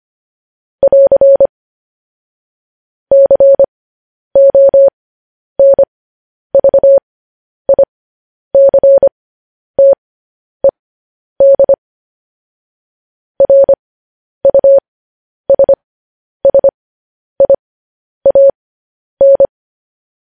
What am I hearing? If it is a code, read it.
Morse code, heard as a keyed tone, about 25 wpm, spaced as for 10 wpm, 550 Hz. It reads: <AR> CONVICTED RUSSIAN
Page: news-headlines-in-morse-code-at-10-wpm